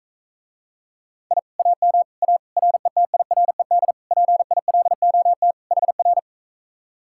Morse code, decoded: I AM A RETIRED PILOT HR

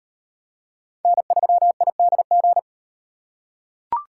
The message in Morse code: N3IDG

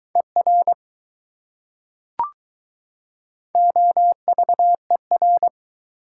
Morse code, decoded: ER OVER

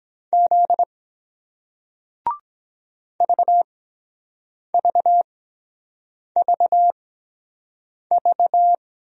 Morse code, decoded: Z VVVV

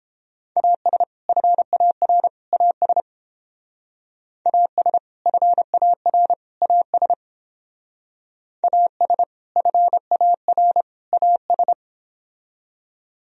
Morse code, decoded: AS FAR AS AS FAR AS AS FAR AS

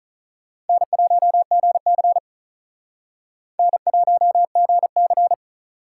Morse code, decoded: N1GC N1GC

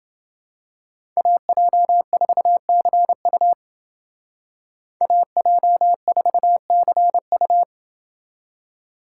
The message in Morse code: AJ4CU AJ4CU

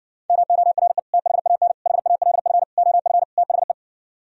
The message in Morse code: MORE THAN HALF OF THE

